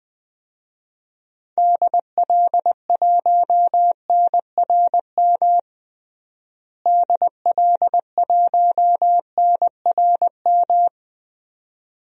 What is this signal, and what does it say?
Morse code, heard as a keyed tone, about 21 wpm, 700 Hz. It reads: DL1NRM DL1NRM
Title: DXCC-100 Call Signs - 6 Characters - Repeated 2x - 20wpm